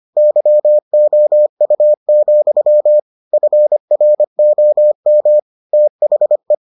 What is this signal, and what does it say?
Morse code, heard as a keyed tone, about 25 wpm, 600 Hz. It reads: YOU, FROM THE